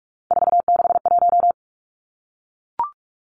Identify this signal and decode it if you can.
Morse code, heard as a keyed tone, about 44 wpm, 700 Hz. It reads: 461
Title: Numbers - 3-Digit 45wpm